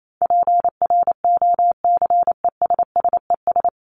Morse code, heard as a keyed tone, about 28 wpm, 700 Hz. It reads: PROCESSES
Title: ES:S Suffix - Top 100 Words with Base Words 28wpm